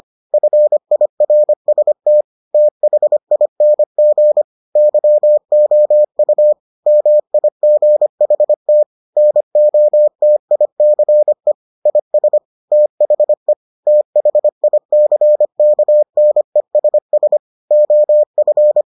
Morse code, heard as a keyed tone, about 25 wpm, 600 Hz. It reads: FIRST THING YOU MIGHT NOTICE IS THE THICKNESS OF